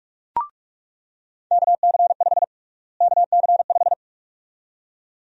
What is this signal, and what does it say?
Morse code, heard as a keyed tone, about 43 wpm, 700 Hz. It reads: KC5 KC5
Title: USA Call Sign Prefixes - Repeated 2x - 45wpm